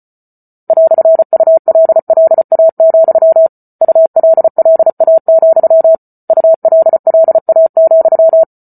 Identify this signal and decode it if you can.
Morse code, heard as a keyed tone, about 34 wpm, 650 Hz. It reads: "ULLA, ULLA, ULLA,